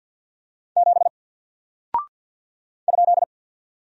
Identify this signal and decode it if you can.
Morse code, heard as a keyed tone, about 48 wpm, 700 Hz. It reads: 7 ?
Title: Single Character (Letter:Number:Punctuation:Prosign) 50wpm